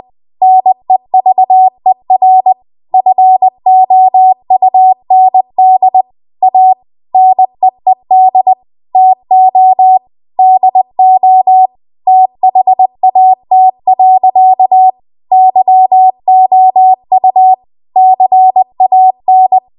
Morse code, heard as a keyed tone, about 20 wpm, 750 Hz. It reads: NEVER FOUND A NEED TO DO THAT. YOU CAN
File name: Archive / 20 WPM / 20WPM